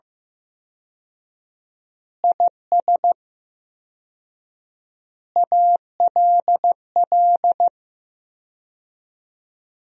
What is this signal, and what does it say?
Morse code, heard as a keyed tone, about 15 wpm, 700 Hz. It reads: IS ALL
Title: 4x Word-Spacing - Sentences from Top 100 Words - 15wpm